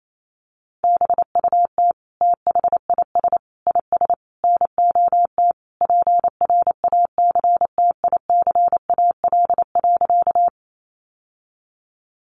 Morse code, heard as a keyed tone, about 28 wpm, 700 Hz. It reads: BUT THIS IS NOT PRACTICAL.
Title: Sentences from Top 1400 Words 28wpm